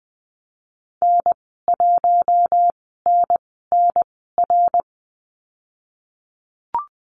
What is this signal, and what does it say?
Morse code, heard as a keyed tone, about 20 wpm, 700 Hz. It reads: N 1 N N R
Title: 076 Lesson 13g - Letter L - Call Signs 20wpm